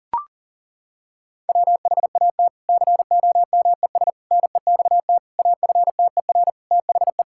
Morse code, heard as a keyed tone, about 40 wpm, 700 Hz. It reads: WHAT COMES NEXT AFTER THE
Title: Sentences from Top 1000 Words - Review Part 5 - 40wpm